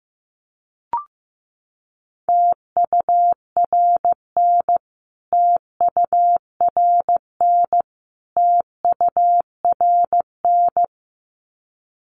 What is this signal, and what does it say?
Morse code, heard as a keyed tone, about 15 wpm, 700 Hz. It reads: TURN TURN TURN
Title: Top 300 Words Repeated 15wpm